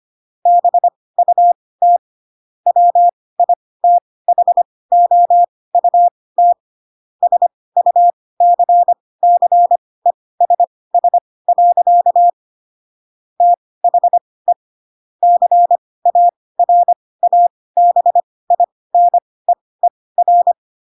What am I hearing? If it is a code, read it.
Morse code, heard as a keyed tone, about 25 wpm, spaced as for 18 wpm, 700 Hz. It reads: BUT WITHOUT SUCCESS. THE CARABINEER
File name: Pinocchio ch3 12 25 ik1hge